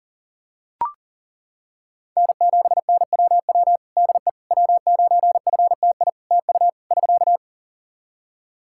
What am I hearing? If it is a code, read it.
Morse code, heard as a keyed tone, about 37 wpm, 700 Hz. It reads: N7NWW DE W9FTI TU <SK>
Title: Practice QSO - Exchange 4 - Ending - 40wpm